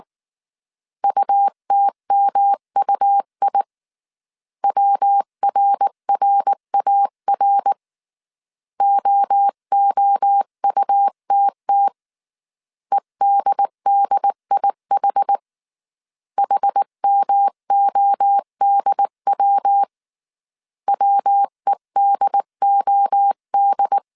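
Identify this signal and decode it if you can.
Morse code, heard as a keyed tone, about 19 wpm, 800 Hz. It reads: UTMUI WRRAR OOUTT EDDIH HMODW WEDOD
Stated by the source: Morse code dictation at 18 wpm from the IW3SGT QRS Morse code training course, five-character groups of 300 characters.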